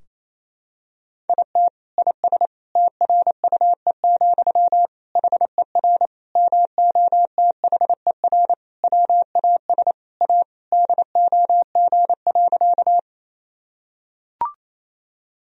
Morse code, heard as a keyed tone, about 28 wpm, 700 Hz. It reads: IT IS TRUE, HER MOTHER WAS A DOG.
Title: Sentences from Top 1000 Words - Review Part 2 - 28wpm